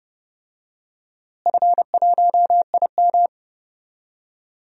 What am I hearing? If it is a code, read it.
Morse code, heard as a keyed tone, about 30 wpm, 700 Hz. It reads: F1IM
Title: DXCC-100 Call Signs - 4 Characters - Repeated 3x - 30wpm